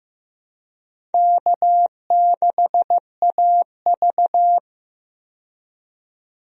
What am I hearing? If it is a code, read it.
Morse code, heard as a keyed tone, about 15 wpm, 700 Hz. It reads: K6AV
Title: USA Call Signs - 4 Characters - 15wpm